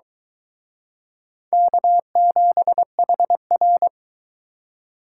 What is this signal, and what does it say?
Morse code, heard as a keyed tone, about 24 wpm, 700 Hz. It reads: K7HR